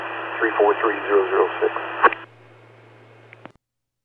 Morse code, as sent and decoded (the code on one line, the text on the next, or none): none